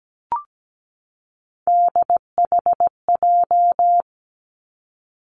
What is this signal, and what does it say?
Morse code, heard as a keyed tone, about 17 wpm, 700 Hz. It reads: DHJ
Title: Three Challenging Letter-Number Combinations 17wpm